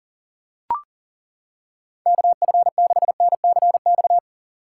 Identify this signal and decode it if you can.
Morse code, heard as a keyed tone, about 40 wpm, 700 Hz. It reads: KF6NCX
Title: K1USN SST QSO Practice - 40wpm